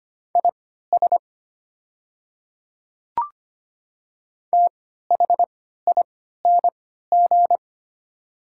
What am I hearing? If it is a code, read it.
Morse code, heard as a keyed tone, about 25 wpm, spaced as for 14 wpm, 700 Hz. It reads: IS THING
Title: 3x Character Spacing - Top 100 Words - 25wpm